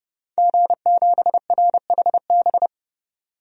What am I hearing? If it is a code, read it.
Morse code, heard as a keyed tone, about 31 wpm, 700 Hz. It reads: G7RHB
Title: DXCC-100 Call Signs - 5 Characters - 30wpm